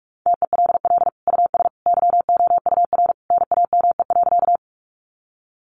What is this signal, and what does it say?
Morse code, heard as a keyed tone, about 44 wpm, 700 Hz. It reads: TELL US YOUR NAME.
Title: Sentences from Top 200 Words 45wpm